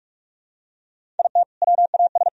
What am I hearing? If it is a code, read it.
Morse code, heard as a keyed tone, about 41 wpm, 700 Hz. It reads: IT WAS
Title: Sentences from Top 1000 Words - Review Part 1 - 45wpm